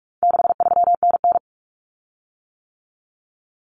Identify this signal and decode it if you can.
Morse code, heard as a keyed tone, about 44 wpm, 700 Hz. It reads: 63NN
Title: Common Cut Numbers - 4-Digit 45wpm